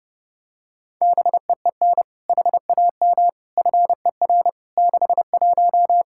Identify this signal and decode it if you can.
Morse code, heard as a keyed tone, about 30 wpm, 700 Hz. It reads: BEEN HAM FER 61
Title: Practice QSO - Exchange 3 - More Info - 30wpm